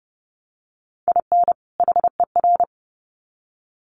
Morse code, heard as a keyed tone, about 30 wpm, 700 Hz. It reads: IN HER
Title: Sets of 2 Words 30wpm